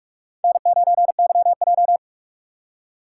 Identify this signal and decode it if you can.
Morse code, heard as a keyed tone, about 44 wpm, 700 Hz. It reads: N9YJ
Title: USA Call Signs - 4 Characters - 2x Character Spacing - 45wpm